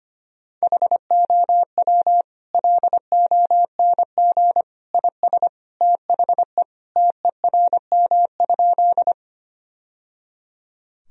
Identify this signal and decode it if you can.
Morse code, heard as a keyed tone, about 25 wpm, 700 Hz. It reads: HOW LONG IS THE TERM?
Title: Sentences from Top 1000 Words - Review Part 4 - 25wpm